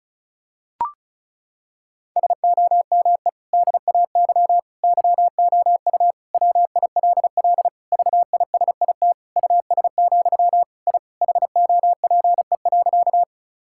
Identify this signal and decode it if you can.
Morse code, heard as a keyed tone, about 36 wpm, 700 Hz. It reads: SOME DAY YOU WILL VISIT US, I HOPE.
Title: Sentences from Top 1000 Words - Review Part 6 - 35wpm